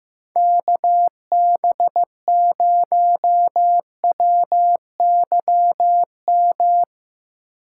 Morse code, heard as a keyed tone, about 16 wpm, 700 Hz. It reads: KB0WYM
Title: Speed Racing - USA 2x3 Call Signs 15wpm